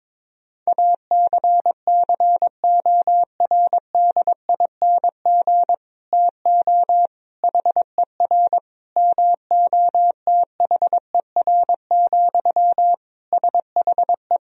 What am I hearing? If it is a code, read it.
Morse code, heard as a keyed tone, about 22 wpm, 700 Hz. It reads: ACCORDING TO HER MOTHER, SHE
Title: Sentences from Top 1000 Words - Review Part 4 - 22wpm